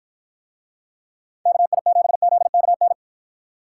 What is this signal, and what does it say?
Morse code, heard as a keyed tone, about 53 wpm, 700 Hz. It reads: KI7ZKN